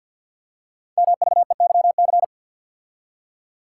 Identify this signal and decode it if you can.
Morse code, heard as a keyed tone, about 50 wpm, 700 Hz. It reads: MVEYC
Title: Code Groups - 5 Letters - 50wpm